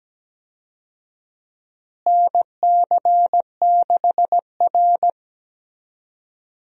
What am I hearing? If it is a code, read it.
Morse code, heard as a keyed tone, about 17 wpm, 700 Hz. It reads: NC6R
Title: USA Call Signs - 4 Characters - Repeated 3x - 17wpm